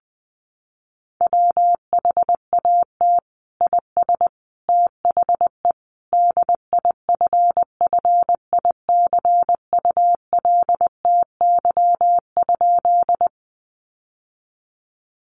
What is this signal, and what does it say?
Morse code, heard as a keyed tone, about 20 wpm, 700 Hz. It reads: WHAT IS THE DIFFICULTY?
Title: Sentences from Top 1500 Words 20wpm